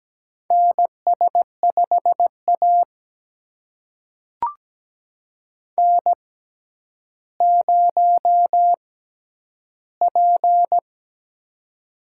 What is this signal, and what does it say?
Morse code, heard as a keyed tone, about 17 wpm, spaced as for 10 wpm, 700 Hz. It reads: NS5A N0P
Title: USA Call Signs - 4 Characters - 6x Character Spacing - 17wpm